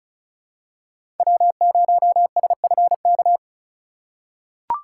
Morse code, heard as a keyed tone, about 36 wpm, 700 Hz. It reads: W0SFK
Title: USA Call Signs - 5 Characters - 4x Character Spacing - 35wpm